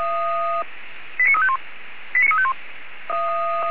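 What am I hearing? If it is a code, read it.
Five-tone selective call (ZVEI), heard as quick runs of five tones, tones of about 80 ms. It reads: 89351 89351
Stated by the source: SDR-received ZVEI five-tone call, code 89347.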